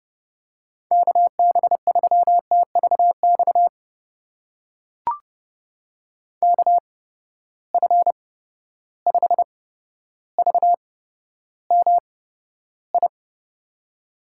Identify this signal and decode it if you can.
Morse code, heard as a keyed tone, about 32 wpm, spaced as for 13 wpm, 700 Hz. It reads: KB3TVX KF5VMI
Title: USA Call Signs - 6 Characters - 8x Character Spacing - 30wpm